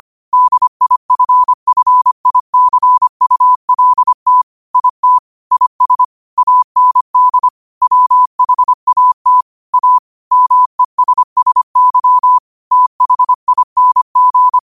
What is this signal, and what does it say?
Morse code, heard as a keyed tone, about 25 wpm, 1000 Hz. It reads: DIFFICULT IT IS AND WHAT A MESSY THING